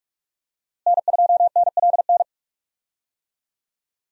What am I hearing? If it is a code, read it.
Morse code, heard as a keyed tone, about 41 wpm, 700 Hz. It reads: N2NLN
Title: USA Call Signs - 5 Characters - Repeated 2x - 45wpm